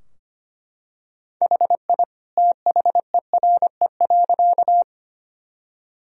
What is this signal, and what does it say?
Morse code, heard as a keyed tone, about 25 wpm, 700 Hz. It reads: HI THERE.
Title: Sentences from Top 1000 Words - Review Part 3 - 25wpm